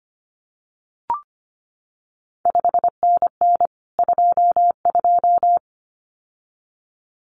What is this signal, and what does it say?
Morse code, heard as a keyed tone, about 25 wpm, 700 Hz. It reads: 5NN 22
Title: CQ WW DX Contest Exchange Encore - 25wpm